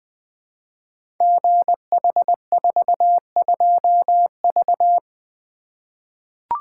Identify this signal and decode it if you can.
Morse code, heard as a keyed tone, about 20 wpm, 700 Hz. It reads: GH42V
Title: Code Groups - 5 Letter:Numbers - 20wpm